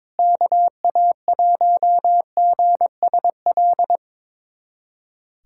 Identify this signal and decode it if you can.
Morse code, heard as a keyed tone, about 22 wpm, 700 Hz. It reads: KA1GSL
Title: USA Callsign DE Callsign - 6 Characters - 22wpm